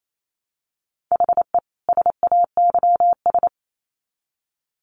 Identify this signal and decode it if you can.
Morse code, heard as a keyed tone, about 28 wpm, 700 Hz. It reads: HE SAYS
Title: Sets of 2 Words 28wpm